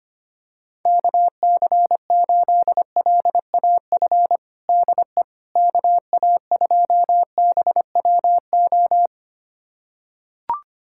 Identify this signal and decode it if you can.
Morse code, heard as a keyed tone, about 25 wpm, 700 Hz. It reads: KC8LAF DE KA2BWO
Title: USA Callsign DE Callsign - 6 Characters - 25wpm